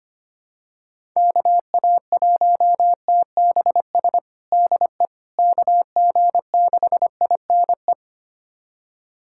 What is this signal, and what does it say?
Morse code, heard as a keyed tone, about 25 wpm, 700 Hz. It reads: KA1TBS DE KG6INE
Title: USA Callsign DE Callsign 25wpm